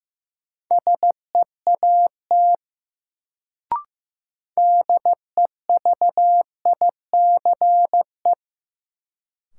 Morse code, heard as a keyed tone, about 15 wpm, 700 Hz. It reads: SEAT DEVICE
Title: Sentences from Top 1500 Words - New Word Drill - 15wpm